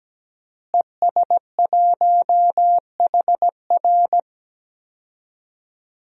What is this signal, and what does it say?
Morse code, heard as a keyed tone, about 17 wpm, 700 Hz. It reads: ES1HR
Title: DXCC-100 Call Signs - 5 Characters - Repeated 3x - 17wpm